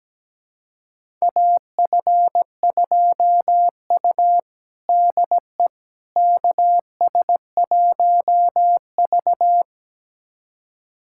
Morse code, heard as a keyed tone, about 17 wpm, 700 Hz. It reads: AF2U DE KS1V
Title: USA Callsign DE Callsign - 4 Characters - 17wpm